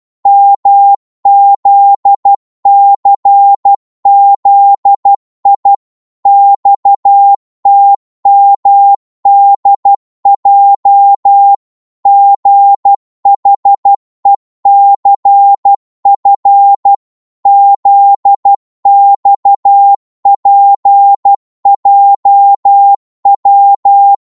Morse code, heard as a keyed tone, about 12 wpm, 800 Hz. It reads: MZCZI XTMDJ GHECF ZXPJW
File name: Morse / ltr-12wpm